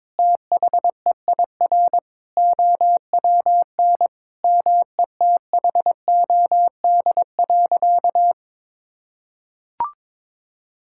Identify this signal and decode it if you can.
Morse code, heard as a keyed tone, about 22 wpm, 700 Hz. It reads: THEIR OWN METHOD.
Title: Sentences from Top 1000 Words - Review Part 6 - 22wpm